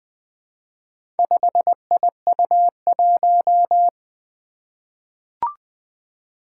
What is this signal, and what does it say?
Morse code, heard as a keyed tone, about 20 wpm, 700 Hz. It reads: 5IU1
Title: Four Challenging Letter-Number Combinations 20wpm